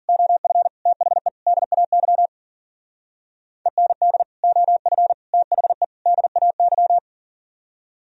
Morse code, heard as a keyed tone, about 50 wpm, 700 Hz. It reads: OF THE DAY END OF THE DAY